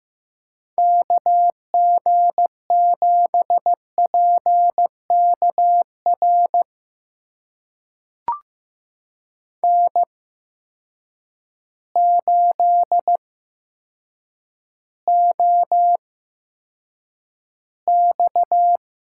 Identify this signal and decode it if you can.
Morse code, heard as a keyed tone, about 15 wpm, spaced as for 8 wpm, 700 Hz. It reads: KG7PKR N8OX
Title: USA Call Signs - 8x Character Spacing - 15wpm